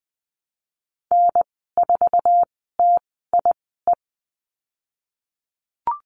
Morse code, heard as a keyed tone, about 20 wpm, spaced as for 15 wpm, 700 Hz. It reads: N4TIE
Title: 055 Lesson 10g - Letter r - Call Signs 20wpm